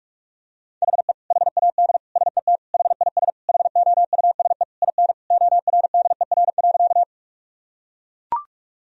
Morse code, heard as a keyed tone, about 41 wpm, 700 Hz. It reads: HE HAD SET HIS HOUSE IN ORDER.
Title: Sentences from Top 1000 Words - Review Part 2 - 45wpm